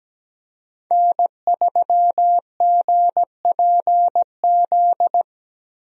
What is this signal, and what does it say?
Morse code, heard as a keyed tone, about 17 wpm, 700 Hz. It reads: N3GPZ
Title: USA Call Signs - 5 Characters - Repeated 2x - 17wpm